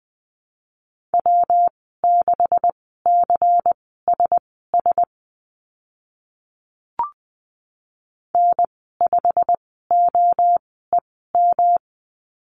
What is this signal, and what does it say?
Morse code, heard as a keyed tone, about 20 wpm, spaced as for 15 wpm, 700 Hz. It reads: W6CSS N5OEM
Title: 140 Lesson 23g - Letter f - Call Signs 20wpm